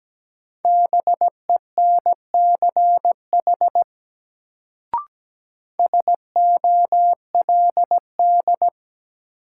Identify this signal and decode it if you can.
Morse code, heard as a keyed tone, about 17 wpm, 700 Hz. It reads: BENCH SOLD
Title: Sentences from Top 1400 Words - New Word Drill Alt - 17wpm